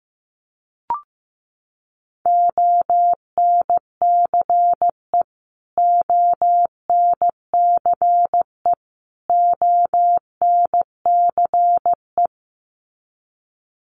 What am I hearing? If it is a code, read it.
Morse code, heard as a keyed tone, about 15 wpm, 700 Hz. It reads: ONCE ONCE ONCE
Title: Top 500 Words Repeated 15wpm